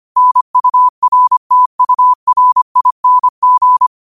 Morse code, heard as a keyed tone, about 25 wpm, 1000 Hz. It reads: NURTURING
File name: qotdfiles / 25wpm